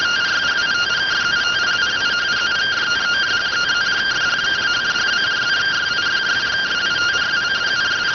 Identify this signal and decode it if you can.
Two-tone teleprinter signal, about 170 Hz shift.